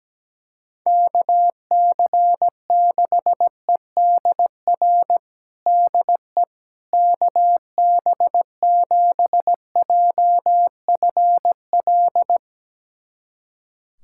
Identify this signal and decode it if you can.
Morse code, heard as a keyed tone, about 17 wpm, 700 Hz. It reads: KC6EDR DE KB7JFL
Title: USA Callsign DE Callsign 17wpm